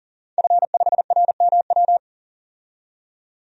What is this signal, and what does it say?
Morse code, heard as a keyed tone, about 40 wpm, 700 Hz. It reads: F5RMW
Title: DXCC-100 Call Signs - 5 Characters - 40wpm